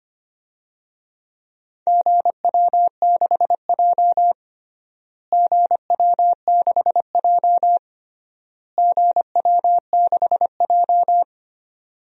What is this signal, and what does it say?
Morse code, heard as a keyed tone, about 25 wpm, 700 Hz. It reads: GW6J GW6J GW6J
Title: DXCC-100 Call Signs - 4 Characters - Repeated 3x - 25wpm